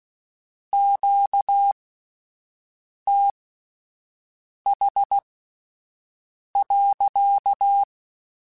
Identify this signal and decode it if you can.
Morse code, heard as a keyed tone, about 16 wpm, spaced as for 5 wpm, 800 Hz. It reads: QTH.
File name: Q Calls / Q Calls-mp3s / Q Calls sppc @10